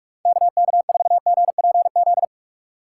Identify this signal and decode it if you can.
Morse code, heard as a keyed tone, about 46 wpm, 700 Hz. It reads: KK4GPZ
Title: USA Call Signs - 6 Characters - 2x Character Spacing - 45wpm